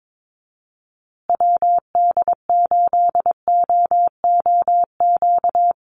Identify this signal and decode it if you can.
Morse code, heard as a keyed tone, about 22 wpm, 700 Hz. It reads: WD8OOQ
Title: USA Call Signs Repeated 2x 22wpm